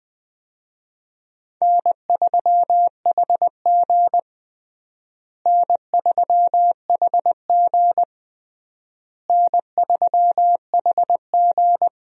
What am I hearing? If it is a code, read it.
Morse code, heard as a keyed tone, about 20 wpm, 700 Hz. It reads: N3HG N3HG N3HG
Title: USA Call Signs - 4 Characters - Repeated 3x - 20wpm